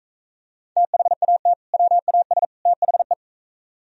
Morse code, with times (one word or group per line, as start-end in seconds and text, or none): 0.8-1.5 THAT
1.7-2.4 WAS
2.7-3.1 THE